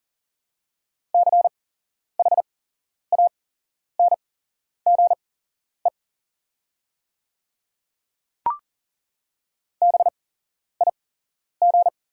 Morse code, heard as a keyed tone, about 40 wpm, spaced as for 11 wpm, 700 Hz. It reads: CHANGE BIG
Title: 8x Character Spacing - Top 200 Words - 40wpm